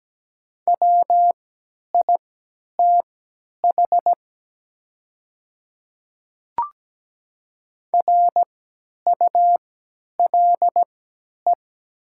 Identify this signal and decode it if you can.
Morse code, heard as a keyed tone, about 17 wpm, spaced as for 10 wpm, 700 Hz. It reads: WITH RULE
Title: 3x Character Spacing - Top 500 Words - 17wpm